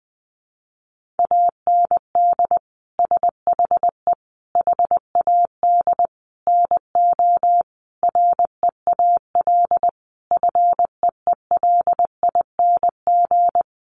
Morse code, heard as a keyed tone, about 20 wpm, 700 Hz. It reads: AND SHE HAD NO REAL FEELING
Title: Sentences from Top 1200 Words 20wpm